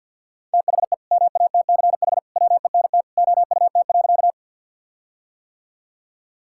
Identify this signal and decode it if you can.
Morse code, heard as a keyed tone, about 48 wpm, 700 Hz. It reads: THE MATCH WENT OUT.
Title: Sentences from Top 1600 Words 50wpm